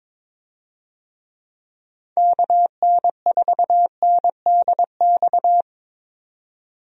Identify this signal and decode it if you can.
Morse code, heard as a keyed tone, about 22 wpm, 700 Hz. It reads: KN4NDX
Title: USA Call Signs - 6 Characters - 22wpm